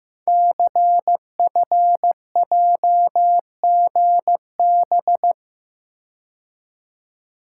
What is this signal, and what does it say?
Morse code, heard as a keyed tone, about 15 wpm, 700 Hz. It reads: CFJGB
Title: Code Groups - 5 Letters - 15wpm